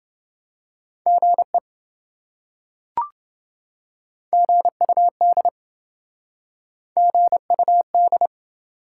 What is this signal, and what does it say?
Morse code, heard as a keyed tone, about 28 wpm, 700 Hz. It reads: GE GUD GUD